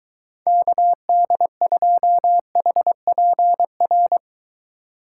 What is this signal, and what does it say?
Morse code, heard as a keyed tone, about 23 wpm, 700 Hz. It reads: KD2HPR